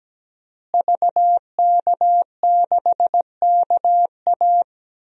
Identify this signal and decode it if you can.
Morse code, heard as a keyed tone, about 17 wpm, 700 Hz. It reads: VK6KA
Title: DXCC-100 Call Signs - 5 Characters - Repeated 2x - 17wpm